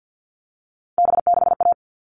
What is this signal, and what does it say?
Morse code, heard as a keyed tone, about 50 wpm, 700 Hz. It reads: B6A
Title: All Cut Numbers - 3-Digit 50wpm